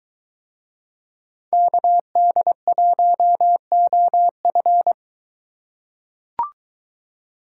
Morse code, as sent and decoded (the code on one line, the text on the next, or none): -.- -.. .---- --- ..-.
KD1OF